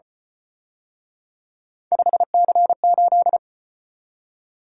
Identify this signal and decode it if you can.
Morse code, heard as a keyed tone, about 32 wpm, 700 Hz. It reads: 5C8